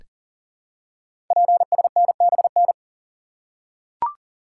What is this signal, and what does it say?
Morse code, heard as a keyed tone, about 38 wpm, 700 Hz. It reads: PSNBN
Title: Code Groups - 5 Letters - 40wpm